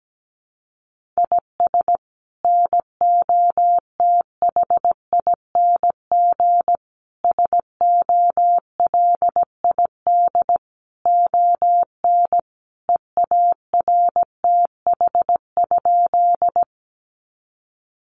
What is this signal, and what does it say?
Morse code, heard as a keyed tone, about 17 wpm, 700 Hz. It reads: IS NOTHING SOLID ON EARTH?
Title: Sentences from Top 1500 Words 17wpm